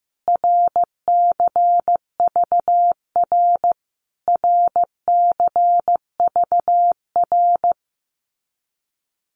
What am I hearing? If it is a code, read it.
Morse code, heard as a keyed tone, about 15 wpm, 700 Hz. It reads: RCVR RCVR
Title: Abbreviations Repeated 3x - 15wpm